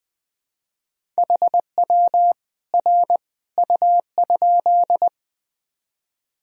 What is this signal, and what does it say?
Morse code, heard as a keyed tone, about 20 wpm, 700 Hz. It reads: HW R U?
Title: Sentences from Top 100 Words and Abbreviations 20wpm